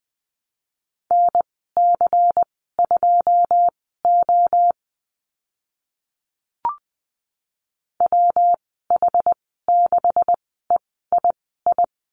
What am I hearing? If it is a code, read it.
Morse code, heard as a keyed tone, about 20 wpm, spaced as for 14 wpm, 700 Hz. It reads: NC2O WH6EII
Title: 161 Lesson 26g - Letter g - Call Signs 20wpm